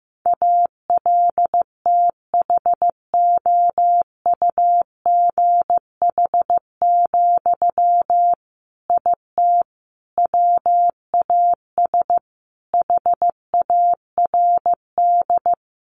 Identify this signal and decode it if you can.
Morse code, heard as a keyed tone, about 15 wpm, 700 Hz. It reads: ALTHOUGH, IT WAS HARD
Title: Sentences from Top 600 Words 15wpm